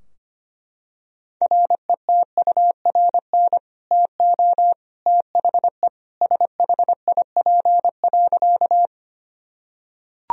Morse code, heard as a keyed tone, about 25 wpm, 700 Hz. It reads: RETURN TO THE SHIP.
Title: Sentences from Top 1000 Words - Review Part 2 - 25wpm